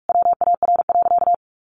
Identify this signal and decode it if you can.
Morse code, heard as a keyed tone, about 44 wpm, 700 Hz. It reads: WAR.
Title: Sentences from Top 1500 Words 45wpm